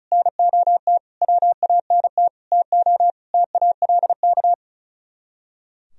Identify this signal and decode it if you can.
Morse code, heard as a keyed tone, about 35 wpm, 700 Hz. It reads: NOT WANT TO TALK
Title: Speed-Racing - Sets of 5 Words 35wpm